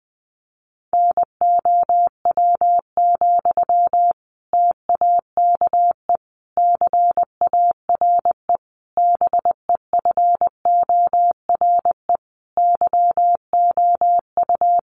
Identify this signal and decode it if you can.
Morse code, heard as a keyed tone, about 20 wpm, 700 Hz. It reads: NOW, TAKE CARE BEFORE YOU
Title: Sentences from Top 1500 Words 20wpm